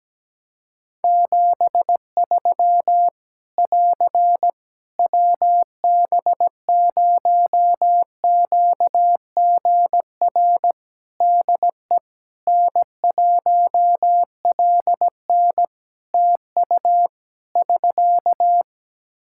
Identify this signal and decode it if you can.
Morse code, heard as a keyed tone, about 17 wpm, 700 Hz. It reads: 73 <AR> WB0QGR DE N1LN TU <SK>
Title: Practice QSO - Exchange 4 - Ending - 17wpm